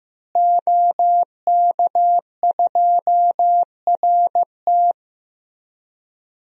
Morse code, heard as a keyed tone, about 15 wpm, 700 Hz. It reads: OK2RT
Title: DXCC-100 Call Signs - 5 Characters - Repeated 3x - 15wpm